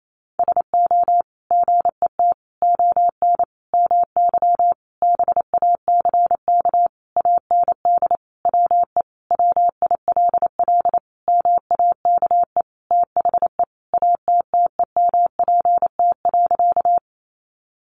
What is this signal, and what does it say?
Morse code, heard as a keyed tone, about 28 wpm, 700 Hz. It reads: SO GET ON MY BACK AND WE WILL MAKE THE ATTEMPT.
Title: Sentences from Top 1000 Words 28wpm